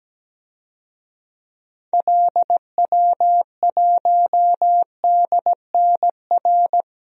Morse code, heard as a keyed tone, about 17 wpm, 700 Hz. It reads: LW1DNR
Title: DXCC-100 Call Signs - 6 Characters - Repeated 2x - 17wpm